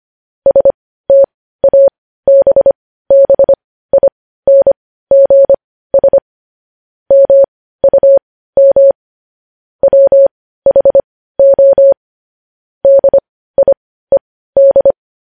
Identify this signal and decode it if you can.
Morse code, heard as a keyed tone, about 25 wpm, spaced as for 15 wpm, 550 Hz. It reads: STABBINGS MUM WHO DIED